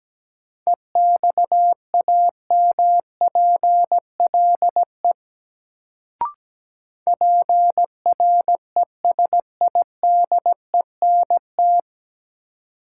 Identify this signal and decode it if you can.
Morse code, heard as a keyed tone, about 17 wpm, 700 Hz. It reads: EXAMPLE PRESIDENT
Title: Sets of 2 Words - New Word Drill - 17wpm